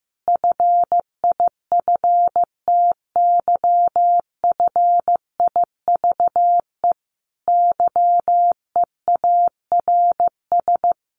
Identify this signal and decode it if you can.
Morse code, heard as a keyed tone, about 15 wpm, 700 Hz. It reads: FIFTYFIVE YEARS
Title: A Sound Of Thunder 15wpm